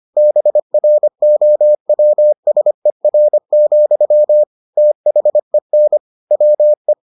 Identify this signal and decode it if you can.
Morse code, heard as a keyed tone, about 25 wpm, 600 Hz. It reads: BROWSER, THEN WE